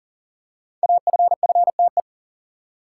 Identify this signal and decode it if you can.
Morse code, heard as a keyed tone, about 41 wpm, 700 Hz. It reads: AFFTE
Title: Code Groups - 5 Letters - 40wpm